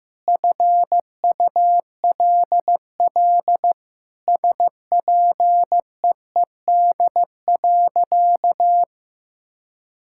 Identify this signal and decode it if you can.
Morse code, heard as a keyed tone, about 16 wpm, 700 Hz. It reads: FULL SPEED.
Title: Sentences from Top 1000 Words - Review Part 5 - 15wpm